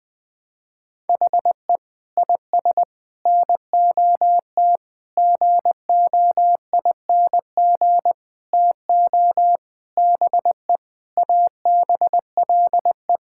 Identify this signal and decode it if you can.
Morse code, heard as a keyed tone, about 20 wpm, 700 Hz. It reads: HE IS NOT GOING TO BE ABLE
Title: Sentences with 5-Word Phrases - 20wpm